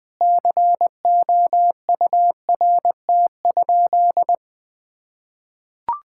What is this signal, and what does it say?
Morse code, heard as a keyed tone, about 20 wpm, 700 Hz. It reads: COURT?
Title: Sentences from Top 1600 Words 20wpm